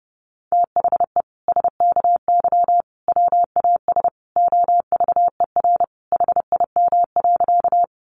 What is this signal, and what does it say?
Morse code, heard as a keyed tone, about 30 wpm, 700 Hz. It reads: THE SKY WAS OVER HIM.
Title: Sentences from Top 1200 Words 30wpm